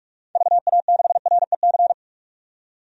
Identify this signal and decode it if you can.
Morse code, heard as a keyed tone, about 44 wpm, 700 Hz. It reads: VA6REC
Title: DXCC-100 Call Signs - 6 Characters - 45wpm